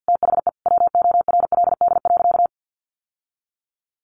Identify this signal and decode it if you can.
Morse code, heard as a keyed tone, about 50 wpm, 700 Hz. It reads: THE WORLD.
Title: Sentences from Top 600 Words 50wpm